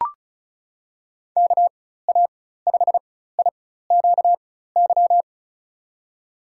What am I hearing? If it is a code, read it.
Morse code, heard as a keyed tone, about 36 wpm, spaced as for 16 wpm, 700 Hz. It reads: KA5IQY
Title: USA Call Signs - 6 Characters - 4x Character Spacing - 35wpm